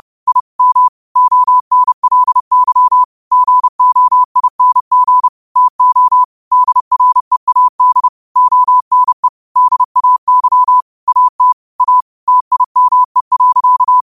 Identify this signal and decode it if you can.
Morse code, heard as a keyed tone, about 30 wpm, 1000 Hz. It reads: I M ONLY GOING TO DREAD ONE DAY AT A TIME.
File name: qotdfiles / 30wpm